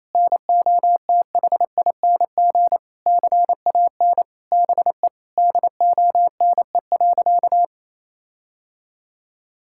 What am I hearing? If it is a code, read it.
Morse code, heard as a keyed tone, about 28 wpm, 700 Hz. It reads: NOTHING CAN BE DONE.
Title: Sentences from Top 1600 Words 28wpm